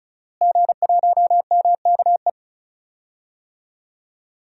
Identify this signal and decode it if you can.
Morse code, heard as a keyed tone, about 36 wpm, 700 Hz. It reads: G1MKE
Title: DXCC-100 Call Signs - 5 Characters - Repeated 3x - 35wpm